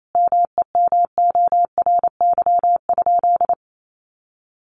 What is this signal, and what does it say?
Morse code, heard as a keyed tone, about 28 wpm, 700 Hz. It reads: MEMORY?
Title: Sentences from Top 1300 Words 28wpm